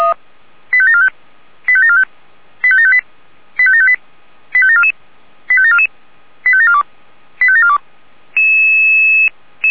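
Five-tone selective call (ZVEI), heard as quick runs of five tones, tones of about 70 ms. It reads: 85745 85745 85758 85758 85744 85744 85742 85742
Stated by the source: ZVEI five-tone selective call received on an SDR.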